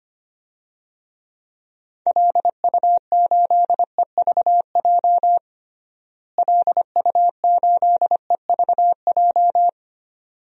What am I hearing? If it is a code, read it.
Morse code, heard as a keyed tone, about 25 wpm, 700 Hz. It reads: LU8EVJ LU8EVJ
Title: DXCC-100 Call Signs - 6 Characters - Repeated 2x - 25wpm